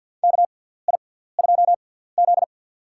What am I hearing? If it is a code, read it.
Morse code, heard as a keyed tone, about 49 wpm, spaced as for 17 wpm, 700 Hz. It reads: KI2Z
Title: USA Call Signs - 4 Characters - 6x Character Spacing - 50wpm